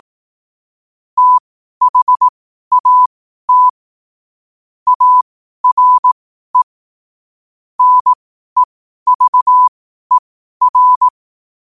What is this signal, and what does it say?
Morse code, heard as a keyed tone, about 18 wpm, spaced as for 12 wpm, 1000 Hz. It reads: THAT ARE NEVER
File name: qotdfiles / 12wpm